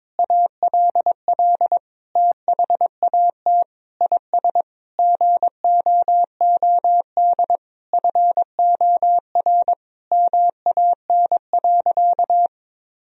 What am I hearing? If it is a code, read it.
Morse code, heard as a keyed tone, about 22 wpm, 700 Hz. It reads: ALL THAT IS GOOD FOR MAN.
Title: Sentences from Top 1000 Words - Review Part 3 - 22wpm